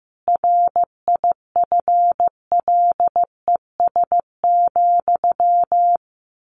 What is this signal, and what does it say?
Morse code, heard as a keyed tone, about 15 wpm, 700 Hz. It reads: RIFLES,
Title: A Sound Of Thunder 15wpm